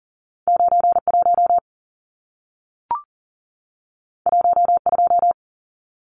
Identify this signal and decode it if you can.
Morse code, heard as a keyed tone, about 40 wpm, 700 Hz. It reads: 91 12
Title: Numbers - 2-Digit 40wpm